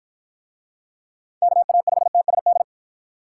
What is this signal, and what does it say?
Morse code, heard as a keyed tone, about 52 wpm, 700 Hz. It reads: KA5TSD